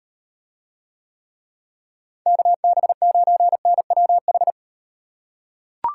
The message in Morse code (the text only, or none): KB9NWH